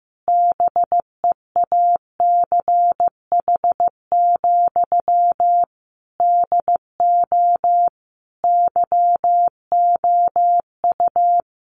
Morse code, heard as a keyed tone, about 15 wpm, 700 Hz. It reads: BEACH, DO YOU
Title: Sentences from Top 1500 Words 15wpm